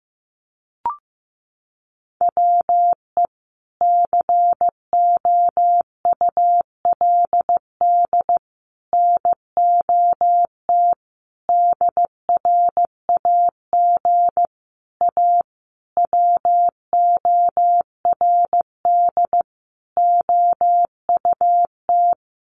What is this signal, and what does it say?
Morse code, heard as a keyed tone, about 15 wpm, 700 Hz. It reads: WE COULD NOT DRAG A WORD OUT
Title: Sentences from Top 1500 Words 15wpm